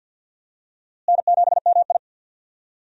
Morse code, heard as a keyed tone, about 50 wpm, 700 Hz. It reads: N7MI
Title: USA Call Signs - 4 Characters - Repeated 2x - 50wpm